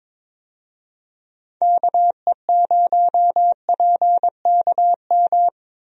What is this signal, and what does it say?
Morse code, heard as a keyed tone, about 22 wpm, 700 Hz. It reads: KE0PKM
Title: USA Call Signs - 6 Characters - Repeated 3x - 22wpm